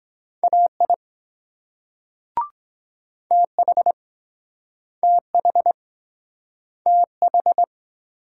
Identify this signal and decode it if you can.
Morse code, heard as a keyed tone, about 23 wpm, 700 Hz. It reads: AI TH TH TH